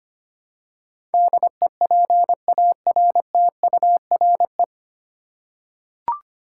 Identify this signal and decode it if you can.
Morse code, heard as a keyed tone, about 25 wpm, 700 Hz. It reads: DEPARTURE
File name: Nine Letter Words - 25wpm